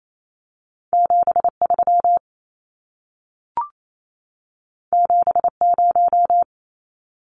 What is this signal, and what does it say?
Morse code, heard as a keyed tone, about 28 wpm, 700 Hz. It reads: 73 70
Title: Numbers - 2-Digit 28wpm